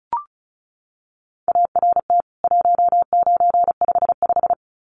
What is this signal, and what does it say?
Morse code, heard as a keyed tone, about 34 wpm, 700 Hz. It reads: ART 1955
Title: CWT Exchange 35wpm